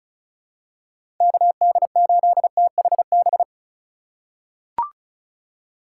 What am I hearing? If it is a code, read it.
Morse code, heard as a keyed tone, about 36 wpm, 700 Hz. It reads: KD8THB
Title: USA Call Signs - 6 Characters - 2x Character Spacing - 35wpm